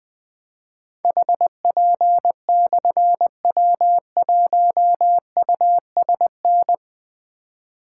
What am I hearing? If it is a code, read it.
Morse code, heard as a keyed tone, about 20 wpm, 700 Hz. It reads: HP/W1USN
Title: Difficult Call Signs 20wpm